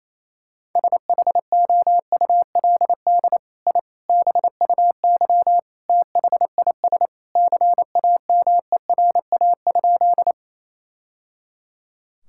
Morse code, heard as a keyed tone, about 28 wpm, 700 Hz. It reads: SHOULD I BUY THIS CAMERA?
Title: Sentences from Top 1000 Words - Review Part 5 - 28wpm